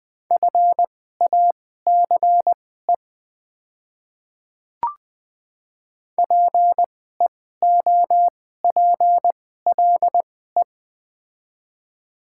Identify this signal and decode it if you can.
Morse code, heard as a keyed tone, about 20 wpm, spaced as for 15 wpm, 690 Hz. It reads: FACE PEOPLE
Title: 2x Character Spacing - Top 300 Words - 20wpm